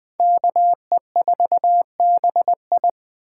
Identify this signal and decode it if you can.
Morse code, heard as a keyed tone, about 20 wpm, 700 Hz. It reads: KE4BI
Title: Speed Racing - USA Call Signs 17wpm